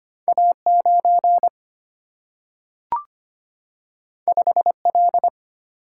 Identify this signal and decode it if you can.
Morse code, heard as a keyed tone, about 25 wpm, 700 Hz. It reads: A9 5L
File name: Two Letter-Number Combinations - 25wpm